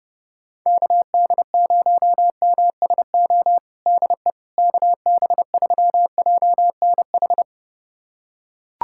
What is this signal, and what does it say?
Morse code, heard as a keyed tone, about 30 wpm, 700 Hz. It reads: KD0MSO DE KB3JNH
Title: USA Callsign DE Callsign - 6 Characters - 30wpm